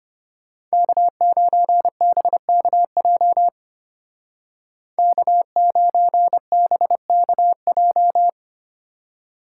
Morse code, heard as a keyed tone, about 31 wpm, 700 Hz. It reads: K9BKJ K9BKJ
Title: Speed Racing - USA 1x3 Call Signs 25wpm